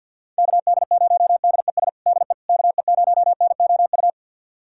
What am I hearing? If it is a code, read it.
Morse code, heard as a keyed tone, about 52 wpm, 700 Hz. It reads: KD0DES DE KE0NOU
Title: USA Callsign DE Callsign - 6 Characters - 50wpm